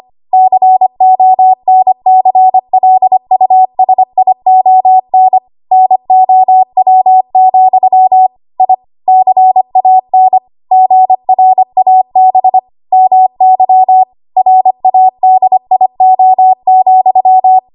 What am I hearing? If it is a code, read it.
Morse code, heard as a keyed tone, about 25 wpm, 750 Hz. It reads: CONCLUSION NOW, I CAN GRAB MY RADIO,
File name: Morse / 25WPM